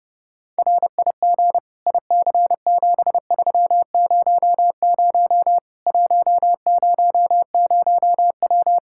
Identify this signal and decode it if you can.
Morse code, heard as a keyed tone, about 32 wpm, 700 Hz. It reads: RIG IC7300 100W
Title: Practice QSO - Exchange 2 - Info - 30wpm